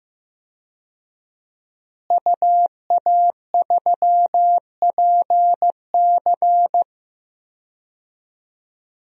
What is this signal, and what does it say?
Morse code, heard as a keyed tone, about 15 wpm, 700 Hz. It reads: UA3PC
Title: DXCC-100 Call Signs - 5 Characters - Repeated 3x - 15wpm